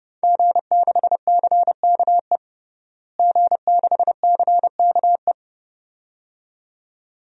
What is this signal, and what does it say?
Morse code, heard as a keyed tone, about 31 wpm, 700 Hz. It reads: G6CKE G6CKE
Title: DXCC-100 Call Signs - 5 Characters - Repeated 2x - 30wpm